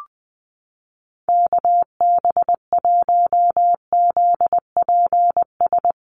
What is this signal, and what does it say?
Morse code, heard as a keyed tone, about 20 wpm, 700 Hz. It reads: KB1ZPS
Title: USA Call Signs Repeated 2x 20wpm